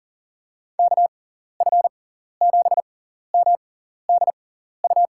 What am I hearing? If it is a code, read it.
Morse code, heard as a keyed tone, about 40 wpm, spaced as for 14 wpm, 700 Hz. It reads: KF7MDU
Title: USA Call Signs - 6 Characters - 6x Character Spacing - 40wpm